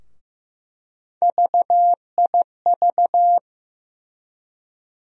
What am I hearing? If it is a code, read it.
Morse code, heard as a keyed tone, about 16 wpm, 700 Hz. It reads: VIV
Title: USA Call Sign Postfixes - 3 Letters 15wpm